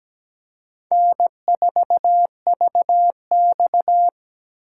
Morse code, heard as a keyed tone, about 17 wpm, 700 Hz. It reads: N4VX
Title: USA Call Signs - 4 Characters - 2x Character Spacing - 17wpm